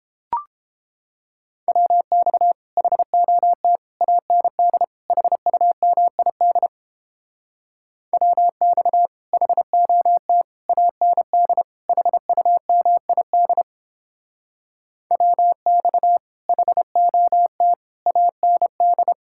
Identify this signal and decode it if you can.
Morse code, heard as a keyed tone, about 29 wpm, 700 Hz. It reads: WX HOT AND HUMID WX HOT AND HUMID WX HOT AND